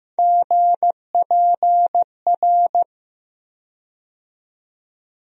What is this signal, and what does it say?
Morse code, heard as a keyed tone, about 15 wpm, 700 Hz. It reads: GPR
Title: USA Call Sign Postfixes - 3 Letters 15wpm